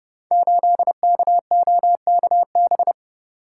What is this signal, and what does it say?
Morse code, heard as a keyed tone, about 28 wpm, 700 Hz. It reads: 8KOKB
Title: Code Groups - 5 Letter:Numbers - 30wpm